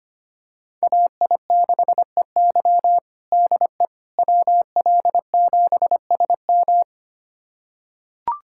Morse code, heard as a keyed tone, about 25 wpm, 700 Hz. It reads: AI6EY DE WL7SM
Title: USA Callsign DE Callsign - 5 Characters - 25wpm